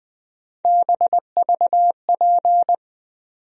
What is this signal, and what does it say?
Morse code, heard as a keyed tone, about 20 wpm, 700 Hz. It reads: BVP
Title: USA Call Sign Postfixes - 3 Letters 20wpm